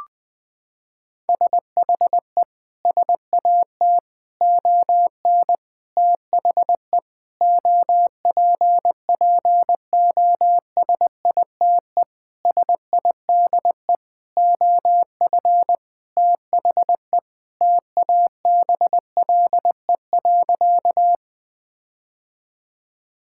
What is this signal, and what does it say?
Morse code, heard as a keyed tone, about 20 wpm, 700 Hz. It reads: SHE SAT ON THE OPPOSITE SIDE OF THE TABLE.
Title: Sentences with 6-Word Phrases - 20wpm